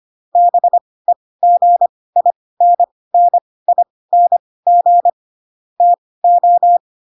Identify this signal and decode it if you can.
Morse code, heard as a keyed tone, about 25 wpm, spaced as for 18 wpm, 700 Hz. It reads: BEGINNING TO
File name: Pinocchio / Pinocchio ch4 12 25 ik1hge